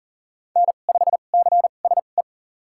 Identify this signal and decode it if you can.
Morse code, heard as a keyed tone, about 40 wpm, spaced as for 29 wpm, 700 Hz. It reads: N5CSE
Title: USA Call Signs - 5 Characters - 2x Character Spacing - 40wpm